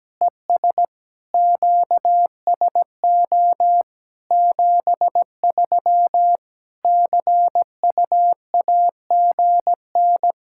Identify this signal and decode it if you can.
Morse code, heard as a keyed tone, about 17 wpm, 700 Hz. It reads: ES QSO 73 CUAGN
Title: Practice QSO - Exchange 4 - Ending - 17wpm